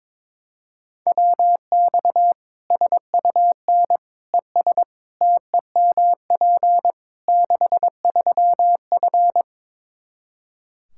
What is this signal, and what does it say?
Morse code, heard as a keyed tone, about 22 wpm, 700 Hz. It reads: WX SUN ES TEMP 63F
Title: Practice QSO - Exchange 2 - Info - 22wpm